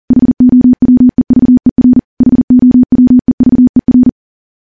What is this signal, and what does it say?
Morse code, heard as a keyed tone, about 40 wpm, 250 Hz. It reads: HOWEVER HOWEVER
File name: morse / morse-training-40WPM